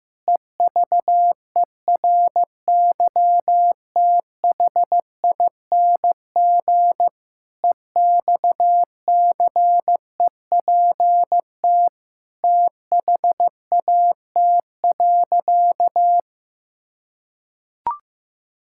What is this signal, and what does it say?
Morse code, heard as a keyed tone, about 16 wpm, 700 Hz. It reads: EVERYTHING EXCEPT THAT.
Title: Sentences from Top 1600 Words 15wpm